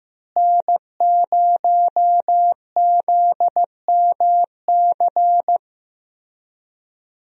Morse code, heard as a keyed tone, about 15 wpm, 700 Hz. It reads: N0ZMC
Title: USA Call Signs - 5 Characters - Repeated 3x - 15wpm